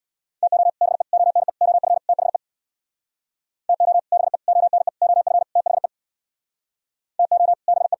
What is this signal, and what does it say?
Morse code, heard as a keyed tone, about 64 wpm, 700 Hz. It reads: TO BE ONE OF THE TO BE ONE OF THE TO BE